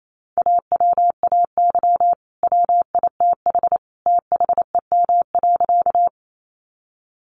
Morse code, heard as a keyed tone, about 28 wpm, 700 Hz. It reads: AWAY WITH THEM.
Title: Sentences from Top 600 Words 28wpm